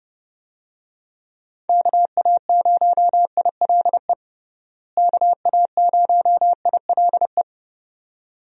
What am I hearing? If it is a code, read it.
Morse code, heard as a keyed tone, about 30 wpm, 700 Hz. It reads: KA0ILE KA0ILE
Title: USA Call Signs - 6 Characters - Repeated 2x - 30wpm